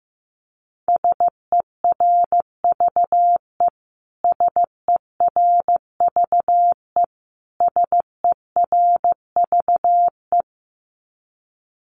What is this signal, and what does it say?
Morse code, heard as a keyed tone, about 15 wpm, 700 Hz. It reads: SERVE SERVE SERVE
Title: Top 500 Words Repeated 15wpm